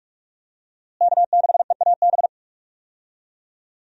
Morse code, heard as a keyed tone, about 46 wpm, 700 Hz. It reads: K6EAB
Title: USA Call Signs - 5 Characters - Repeated 2x - 45wpm